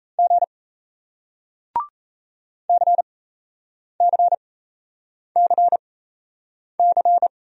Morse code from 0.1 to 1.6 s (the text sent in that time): G